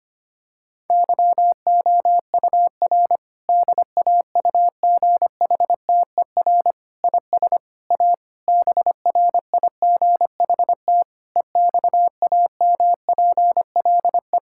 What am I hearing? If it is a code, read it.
Morse code, heard as a keyed tone, about 25 wpm, 700 Hz. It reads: YOUR DAUGHTER IS A BRIGHT EXAMPLE
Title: Sentences from Top 1600 Words 25wpm